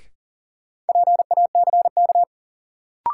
Morse code, heard as a keyed tone, about 40 wpm, 700 Hz. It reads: PACK
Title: Sentences from Top 1300 Words - New Word Drill Alt - 40wpm